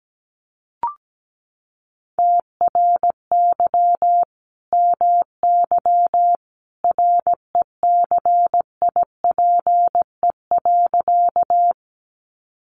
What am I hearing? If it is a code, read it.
Morse code, heard as a keyed tone, about 17 wpm, 700 Hz. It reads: TRY MY RECIPE.
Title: Sentences from Top 1200 Words 17wpm